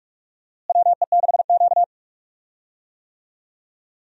Morse code, heard as a keyed tone, about 41 wpm, 700 Hz. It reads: WE6Q
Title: USA Call Signs - 4 Characters - Repeated 2x - 45wpm